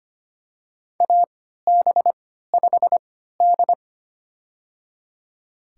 Morse code, heard as a keyed tone, about 25 wpm, spaced as for 14 wpm, 700 Hz. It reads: AB5D
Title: USA Call Signs - 4 Characters - 3x Character Spacing - 25wpm